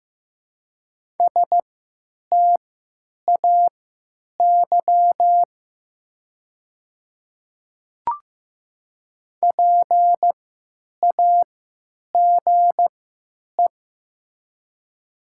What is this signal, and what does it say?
Morse code, heard as a keyed tone, about 15 wpm, spaced as for 9 wpm, 700 Hz. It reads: STAY PAGE
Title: 3x Character Spacing - Top 500 Words - 15wpm